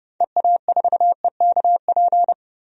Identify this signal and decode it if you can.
Morse code, heard as a keyed tone, about 32 wpm, 700 Hz. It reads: EA4EKP
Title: DXCC-100 Call Signs - 6 Characters - 30wpm